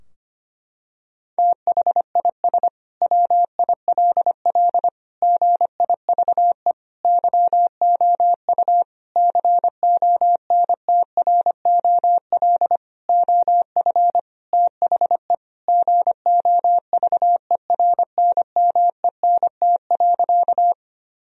Morse code, heard as a keyed tone, about 25 wpm, 700 Hz. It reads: THIS WILL GIVE YOU CONTROL OF THE GOVERNMENT.
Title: Sentences from Top 1000 Words - Review Part 3 - 25wpm